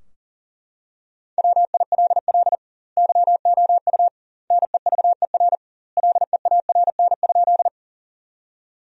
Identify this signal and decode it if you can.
Morse code, heard as a keyed tone, about 37 wpm, 700 Hz. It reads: WILL YOU NEVER LEARN?
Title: 2x Word-Spacing - Sentences from Top 300 Words - 40wpm